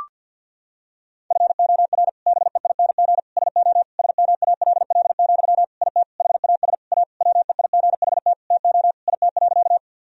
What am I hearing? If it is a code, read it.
Morse code, heard as a keyed tone, about 52 wpm, 700 Hz. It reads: FOR BEING SO SMALL, IT HAS A WEIGHT TO IT.
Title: Sentences from Top 1000 Words - Review Part 3 - 50wpm